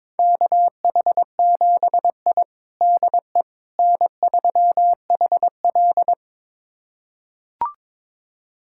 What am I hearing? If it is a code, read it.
Morse code, heard as a keyed tone, about 22 wpm, 700 Hz. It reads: KH7I DE N3HL
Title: USA Callsign DE Callsign - 4 Characters - 22wpm